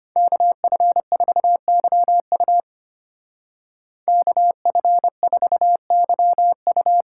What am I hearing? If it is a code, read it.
Morse code, heard as a keyed tone, about 32 wpm, 700 Hz. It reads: KF4YU KF4YU